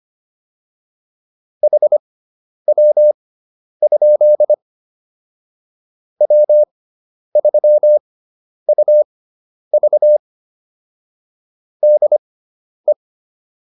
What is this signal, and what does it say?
Morse code, heard as a keyed tone, about 25 wpm, spaced as for 10 wpm, 600 Hz. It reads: HW? W3UV DE